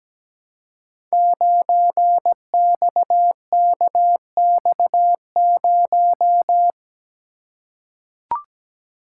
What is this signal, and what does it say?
Morse code, heard as a keyed tone, about 17 wpm, 700 Hz. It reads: 9XKX0
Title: Code Groups - 5 Letter:Numbers - 17wpm